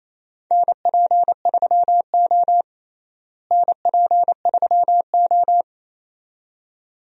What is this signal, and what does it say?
Morse code, heard as a keyed tone, about 28 wpm, 700 Hz. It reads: NP3O NP3O
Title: USA Call Signs - 4 Characters - Repeated 2x - 28wpm